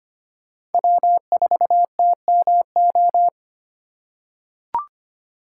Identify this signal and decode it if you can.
Morse code, heard as a keyed tone, about 25 wpm, 700 Hz. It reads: W4TMO
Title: K1USN SST QSO Practice - 25wpm